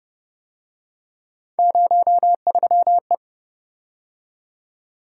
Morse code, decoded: 03E

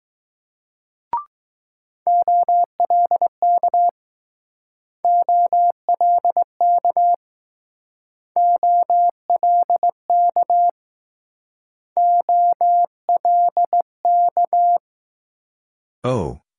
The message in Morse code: OLK OLK OLK OLK

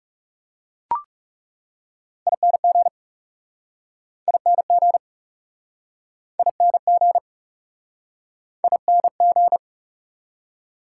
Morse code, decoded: ING ING ING ING

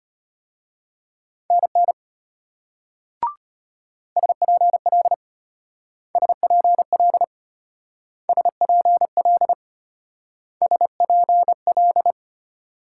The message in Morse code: NN SPL SPL SPL SPL

